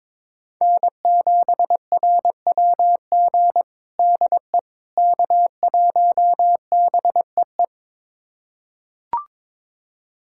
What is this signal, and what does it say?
Morse code, heard as a keyed tone, about 22 wpm, 700 Hz. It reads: N7RWG DE K1BEE
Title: USA Callsign DE Callsign - 5 Characters - 22wpm